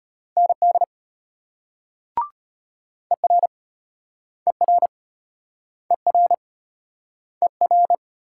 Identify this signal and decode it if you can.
Morse code, heard as a keyed tone, about 35 wpm, 700 Hz. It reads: ND ER ER ER ER